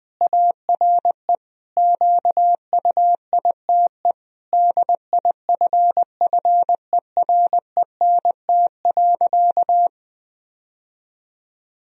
Morse code, heard as a keyed tone, about 21 wpm, 700 Hz. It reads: ARE QUITE DIFFERENT.
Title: Sentences from Top 1000 Words - Review Part 3 - 20wpm